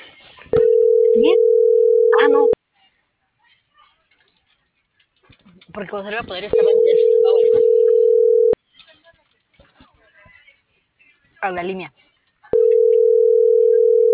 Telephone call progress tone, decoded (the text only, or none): ringback tone